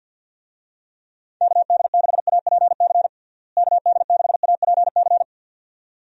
Morse code, heard as a keyed tone, about 52 wpm, 700 Hz. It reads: KD6APC KD6APC
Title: USA Call Signs - 6 Characters - Repeated 2x - 50wpm